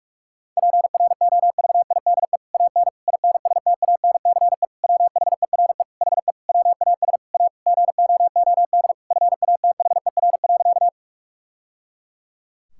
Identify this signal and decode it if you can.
Morse code, heard as a keyed tone, about 41 wpm, 700 Hz. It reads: PROVIDE AN INSTANCE WHERE HE WAS A GOOD FATHER.
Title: Sentences from Top 1000 Words - Review Part 5 - 45wpm